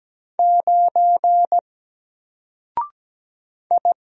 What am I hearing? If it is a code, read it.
Morse code, heard as a keyed tone, about 17 wpm, 700 Hz. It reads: 9 I
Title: Single Letter-Number Alt 17wpm